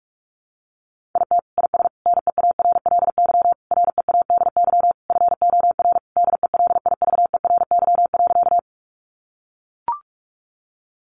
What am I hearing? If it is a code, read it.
Morse code, heard as a keyed tone, about 44 wpm, 700 Hz. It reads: IT IS NEARLY READY FOR DELIVERY.
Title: Sentences from Top 1500 Words 45wpm